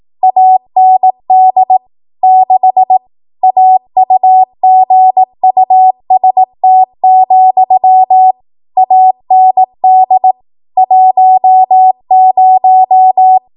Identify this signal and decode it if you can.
Morse code, heard as a keyed tone, about 18 wpm, 750 Hz. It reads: AND 6 AUGUST, AND 10
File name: Morse / 18WPM